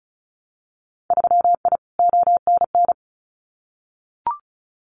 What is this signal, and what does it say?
Morse code, heard as a keyed tone, about 34 wpm, 700 Hz. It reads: 3I ONN
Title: Winter Field Day Exchanges Encore 35wpm